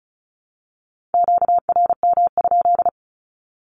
Morse code, heard as a keyed tone, about 36 wpm, 700 Hz. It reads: QRM?
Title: Q-Codes with Spoken Meaning - 35wpm